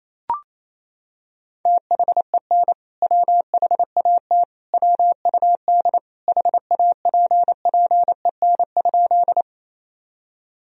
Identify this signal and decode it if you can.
Morse code, heard as a keyed tone, about 28 wpm, 700 Hz. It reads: THEN WHAT WUD HAPPEN?
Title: Sentences from Top 400 Words and Abbreviations 28wpm